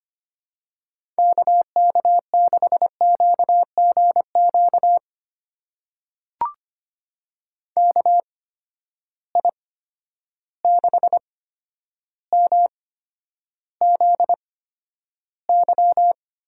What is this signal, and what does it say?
Morse code, heard as a keyed tone, about 25 wpm, spaced as for 11 wpm, 700 Hz. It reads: KK6QGQ KI6MZY
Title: USA Call Signs - 6 Characters - 8x Character Spacing - 25wpm